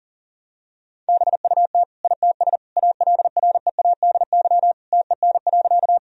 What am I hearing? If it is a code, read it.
Morse code, heard as a keyed tone, about 40 wpm, 700 Hz. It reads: BUT ITS ALREADY TEN.
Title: Sentences from Top 1000 Words - Review Part 2 - 40wpm